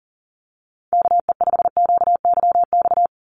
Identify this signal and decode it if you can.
Morse code, heard as a keyed tone, about 40 wpm, 700 Hz. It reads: KE5QYX
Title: USA Call Signs 40wpm